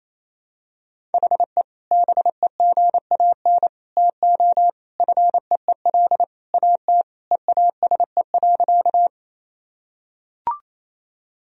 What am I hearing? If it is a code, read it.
Morse code, heard as a keyed tone, about 28 wpm, 700 Hz. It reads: HE BEGAN TO FEEL AT EASE.
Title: Sentences from Top 1000 Words - Review Part 1 - 28wpm